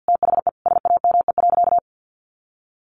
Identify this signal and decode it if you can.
Morse code, heard as a keyed tone, about 50 wpm, 700 Hz. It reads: THE SAME.
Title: Sentences from Top 300 Words 50wpm